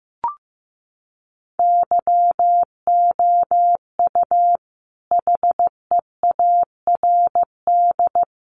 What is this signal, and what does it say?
Morse code, heard as a keyed tone, about 15 wpm, 700 Hz. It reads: YOU HEARD
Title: Sentences from Top 500 Words 15wpm